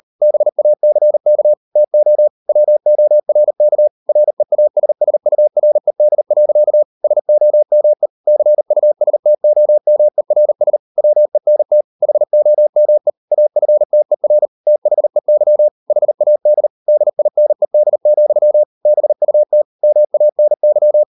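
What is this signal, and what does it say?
Morse code, heard as a keyed tone, about 40 wpm, 600 Hz. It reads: BACK TO WORK REASSURED. SOME CUSTOMERS WENT HOME AFTER THEY HAD DINED, BUT MANY
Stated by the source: Morse code practice at 38 WPM, sending Ken Follett's novel Never in English, from CW med Gnister.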